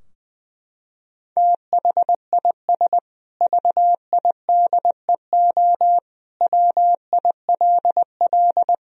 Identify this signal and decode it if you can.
Morse code, heard as a keyed tone, about 20 wpm, 700 Hz. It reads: THIS VIDEO WILL
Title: Sentences from Top 1000 Words - Review Part 4 - 20wpm